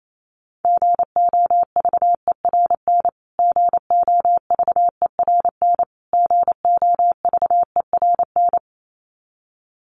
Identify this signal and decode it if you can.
Morse code, heard as a keyed tone, about 28 wpm, 700 Hz. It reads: GOVERN GOVERN GOVERN
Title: Top 500 Words Repeated 28wpm